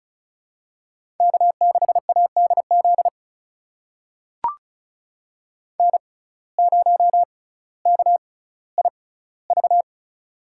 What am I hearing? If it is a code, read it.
Morse code, heard as a keyed tone, about 36 wpm, spaced as for 18 wpm, 700 Hz. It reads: K6ADZ N0KIV
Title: USA Call Signs - 5 Characters - 6x Character Spacing - 35wpm